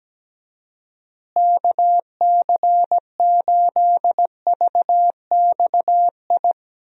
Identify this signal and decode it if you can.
Morse code, heard as a keyed tone, about 17 wpm, 700 Hz. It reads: KC8VXI
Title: USA Call Signs - 6 Characters - Repeated 3x - 17wpm